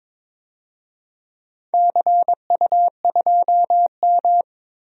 Morse code, heard as a keyed tone, about 22 wpm, 700 Hz. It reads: CU2M
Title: DXCC-100 Call Signs - 4 Characters - 22wpm